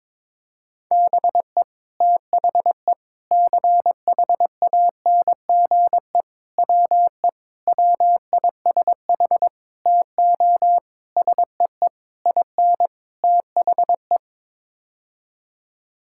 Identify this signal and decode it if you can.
Morse code, heard as a keyed tone, about 22 wpm, 700 Hz. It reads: BE THE CHANGE WE WISH TO SEE IN THE
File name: Sets of 9 Words - 22wpm